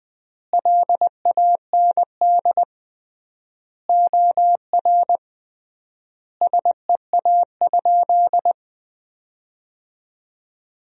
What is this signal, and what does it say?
Morse code, heard as a keyed tone, about 20 wpm, 700 Hz. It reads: LAND OR SEA?
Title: 3x Word-Spacing - Sentences from Top 500 Words - 20wpm